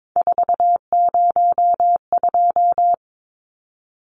Morse code, heard as a keyed tone, about 22 wpm, 700 Hz. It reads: 402
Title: Numbers - 3-Digit 22wpm